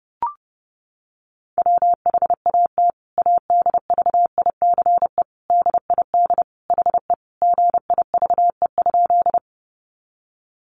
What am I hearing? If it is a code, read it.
Morse code, heard as a keyed tone, about 30 wpm, 700 Hz. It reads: WHAT ADVICE DID HE GIVE?
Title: Sentences from Top 1100 Words 30wpm